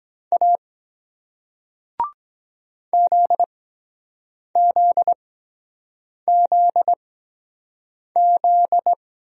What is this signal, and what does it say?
Morse code, heard as a keyed tone, about 23 wpm, spaced as for 7 wpm, 700 Hz. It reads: A ZZZZ